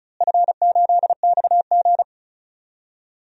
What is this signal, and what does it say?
Morse code, heard as a keyed tone, about 36 wpm, 700 Hz. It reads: F8XG
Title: DXCC-100 Call Signs - 4 Characters - Repeated 2x - 35wpm